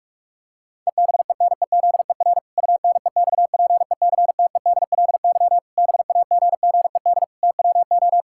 Morse code, heard as a keyed tone, about 46 wpm, 700 Hz. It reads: EBENEZER UNEXPECTEDLY BAGGED TWO
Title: Pangram Challenge v2 - 45wpm